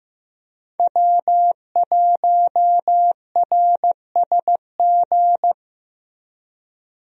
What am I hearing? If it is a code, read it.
Morse code, heard as a keyed tone, about 14 wpm, 700 Hz. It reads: W1RSG
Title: USA Call Signs - 5 Characters - Repeated 3x - 15wpm